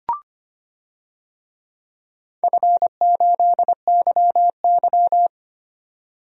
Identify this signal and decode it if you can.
Morse code, heard as a keyed tone, about 25 wpm, 700 Hz. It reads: F8YY
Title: DXCC-100 Call Signs - 4 Characters - 25wpm